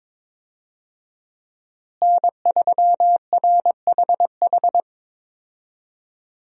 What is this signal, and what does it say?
Morse code, heard as a keyed tone, about 22 wpm, 700 Hz. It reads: N3RHH
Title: USA Call Signs - 5 Characters - Repeated 2x - 22wpm